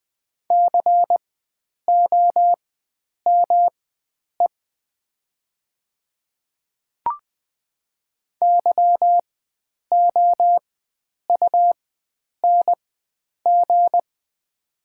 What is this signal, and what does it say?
Morse code, heard as a keyed tone, about 20 wpm, spaced as for 9 wpm, 700 Hz. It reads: COME YOUNG
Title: 4x Character Spacing - Top 500 Words - 20wpm